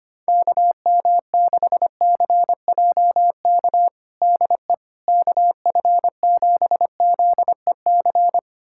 Morse code, heard as a keyed tone, about 25 wpm, 700 Hz. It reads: KM6CJK DE KF7ZEC
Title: USA Callsign DE Callsign - 6 Characters - 25wpm